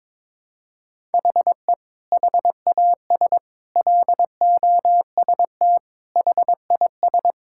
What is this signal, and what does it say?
Morse code, heard as a keyed tone, about 22 wpm, 700 Hz. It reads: HE HAS LOST HIS
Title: Sentences from Top 1600 Words 22wpm